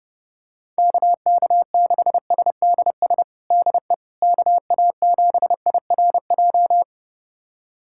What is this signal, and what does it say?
Morse code, heard as a keyed tone, about 30 wpm, 700 Hz. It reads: KK6SDS DE KA7IRJ
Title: USA Callsign DE Callsign - 6 Characters - 30wpm